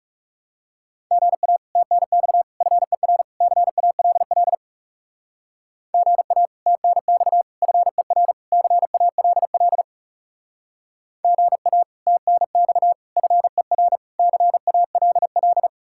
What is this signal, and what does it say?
Morse code, heard as a keyed tone, about 39 wpm, 700 Hz. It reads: GA TNX FER CALL GA TNX FER CALL GA TNX FER CALL